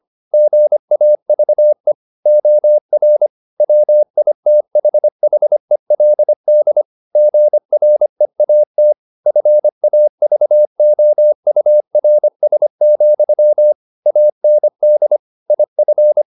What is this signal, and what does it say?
Morse code, heard as a keyed tone, about 25 wpm, 600 Hz. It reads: GAVE OR WITHHELD GREAT FAVOURS, AND IF
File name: the old man and the sea part 1 25wpm